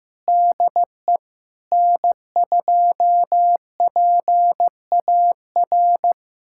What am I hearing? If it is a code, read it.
Morse code, heard as a keyed tone, about 15 wpm, 700 Hz. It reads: DE N2PAR
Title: USA Callsign DE Callsign - 5 Characters - 15wpm